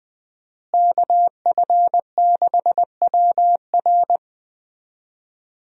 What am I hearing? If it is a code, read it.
Morse code, heard as a keyed tone, about 20 wpm, 700 Hz. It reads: KF6WR
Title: USA Call Signs - 5 Characters - Repeated 2x - 20wpm